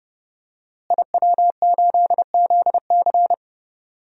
Morse code, heard as a keyed tone, about 30 wpm, 700 Hz. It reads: IW8ZC
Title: DXCC-100 Call Signs - 5 Characters - Repeated 3x - 30wpm